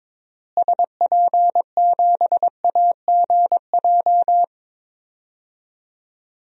Morse code, heard as a keyed tone, about 22 wpm, 700 Hz. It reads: SP7AGJ
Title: DXCC-100 Call Signs - 6 Characters - Repeated 2x - 22wpm